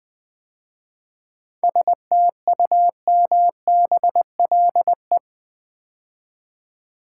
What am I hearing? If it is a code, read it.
Morse code, heard as a keyed tone, about 21 wpm, 700 Hz. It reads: STUMBLE
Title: 7 Letter Words - First 3 Letters Given - 20wpm